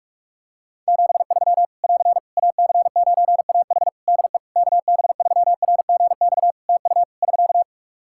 Morse code, heard as a keyed tone, about 48 wpm, 700 Hz. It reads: 73 <AR> AC9AH DE KB3RGX TU <SK>
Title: Practice QSO - Exchange 4 - Ending - 45wpm